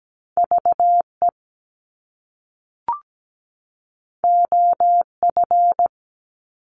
Morse code, heard as a keyed tone, about 17 wpm, 700 Hz. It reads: VE OF
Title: Top Two Letter Combinations in Words 17wpm